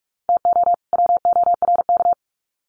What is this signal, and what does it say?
Morse code, heard as a keyed tone, about 44 wpm, 700 Hz. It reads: TO WORK
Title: Sets of 2 Words 45wpm